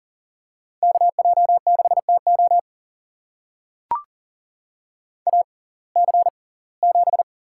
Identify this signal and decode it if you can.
Morse code, heard as a keyed tone, about 40 wpm, spaced as for 24 wpm, 700 Hz. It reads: KJ6TO AC7
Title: USA Call Signs - 5 Characters - 6x Character Spacing - 40wpm